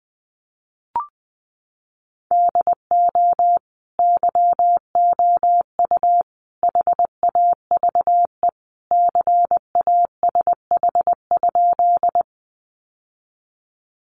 Morse code, heard as a keyed tone, about 20 wpm, 700 Hz. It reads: DO YOU HAVE CASH?
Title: Sentences from Top 700 Words 20wpm